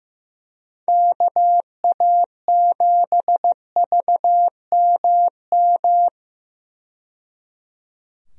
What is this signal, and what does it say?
Morse code, heard as a keyed tone, about 15 wpm, 700 Hz. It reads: KA7VMM
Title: USA Call Signs - 6 Characters - 15wpm